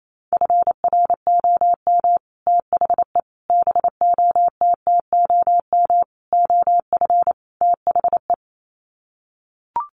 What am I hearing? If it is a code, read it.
Morse code, heard as a keyed tone, about 28 wpm, 700 Hz. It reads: FROM THE BOTTOM OF THE
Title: Sets of 5 Words Encore2 28wpm